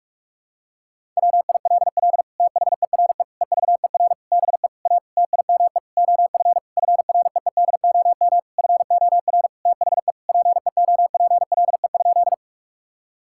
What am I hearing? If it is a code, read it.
Morse code, heard as a keyed tone, about 46 wpm, 700 Hz. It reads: WILL THERE EVER BE A TIME OF FREEDOM FOR THE PEOPLE?
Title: Sentences from Top 1000 Words - Review Part 5 - 45wpm